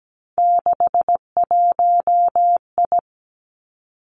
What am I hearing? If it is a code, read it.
Morse code, heard as a keyed tone, about 17 wpm, 700 Hz. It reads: 61I
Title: Three Challenging Letter-Number Combinations 17wpm